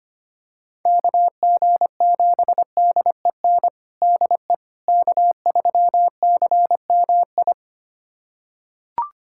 Morse code, heard as a keyed tone, about 25 wpm, 700 Hz. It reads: KG7DEN DE K3CMI
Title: USA Callsign DE Callsign 25wpm